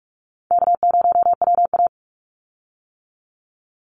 Morse code, heard as a keyed tone, about 44 wpm, 700 Hz. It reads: K0WA
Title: Easy Call Signs - 45wpm